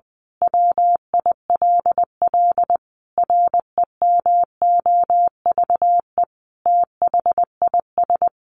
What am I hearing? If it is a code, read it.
Morse code, heard as a keyed tone, about 20 wpm, 700 Hz. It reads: WILL REMOVE THIS
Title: Sentences from Top 1500 Words 20wpm